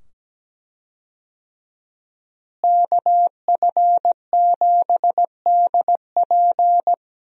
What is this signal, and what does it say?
Morse code, heard as a keyed tone, about 17 wpm, 700 Hz. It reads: KF7DP
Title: USA Call Signs - 5 Characters - Repeated 2x - 17wpm